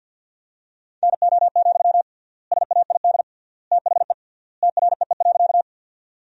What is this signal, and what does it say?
Morse code, heard as a keyed tone, about 53 wpm, 700 Hz. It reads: NO, SAID THE TREE.
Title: 3x Word-Spacing - Sentences from Top 300 Words - 50wpm